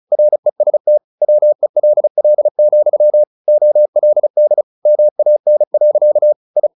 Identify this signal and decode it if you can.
Morse code, heard as a keyed tone, about 34 wpm, 600 Hz. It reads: REST WELL, OLD MAN. I
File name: the old man and the sea 2 35wpm